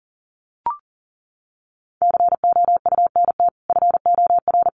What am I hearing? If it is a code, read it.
Morse code, heard as a keyed tone, about 40 wpm, 700 Hz. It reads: COUNT FOR
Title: Sentences from Top 1500 Words 40wpm